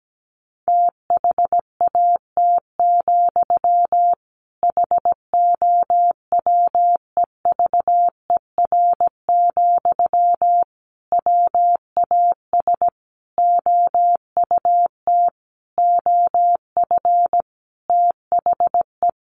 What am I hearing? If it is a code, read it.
Morse code, heard as a keyed tone, about 17 wpm, 700 Hz. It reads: THAT, HOWEVER, WAS OUT OF THE
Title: Sentences from Top 600 Words 17wpm